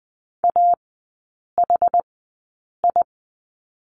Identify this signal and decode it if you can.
Morse code, heard as a keyed tone, about 20 wpm, spaced as for 8 wpm, 700 Hz. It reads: AHI
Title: 107 Lesson 18e - Letter m - Three Characters 20wpm